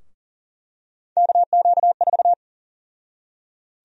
Morse code, heard as a keyed tone, about 40 wpm, 700 Hz. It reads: KQ4
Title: USA Call Sign Prefixes - Repeated 2x - 40wpm